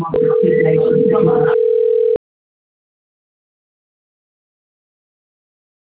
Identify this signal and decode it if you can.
Ringback tone (North America), 440 + 480 Hz.